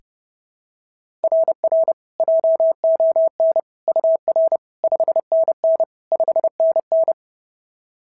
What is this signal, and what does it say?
Morse code, heard as a keyed tone, about 26 wpm, 650 Hz. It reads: RR JON UR 5NN 5NN
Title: SOTA QSO Practice 30wpm